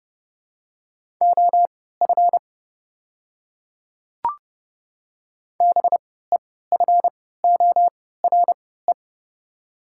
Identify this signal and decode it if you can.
Morse code, heard as a keyed tone, about 30 wpm, spaced as for 17 wpm, 690 Hz. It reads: OF BEFORE
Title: 3x Character Spacing - Top 200 Words - 30wpm